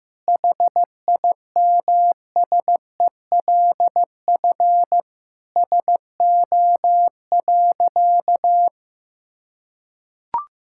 Morse code, heard as a keyed tone, about 16 wpm, 700 Hz. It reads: HIMSELF SO.
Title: Sentences from Top 1600 Words 15wpm